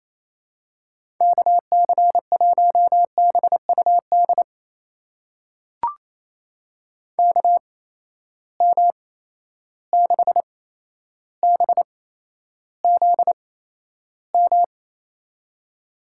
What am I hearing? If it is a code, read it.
Morse code, heard as a keyed tone, about 28 wpm, spaced as for 12 wpm, 700 Hz. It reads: KC1BUD KM6BZM
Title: USA Call Signs - 6 Characters - 8x Character Spacing - 28wpm